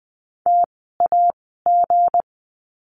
Morse code, heard as a keyed tone, about 20 wpm, spaced as for 14 wpm, 700 Hz. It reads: TAG
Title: 160 Lesson 26f - Letter g - Words 20wpm